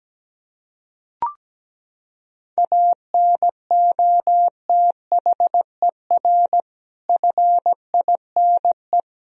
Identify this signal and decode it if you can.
Morse code, heard as a keyed tone, about 17 wpm, 700 Hz. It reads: ANOTHER FINE
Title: Sentences from Top 1000 Words - Review Part 1 - 17wpm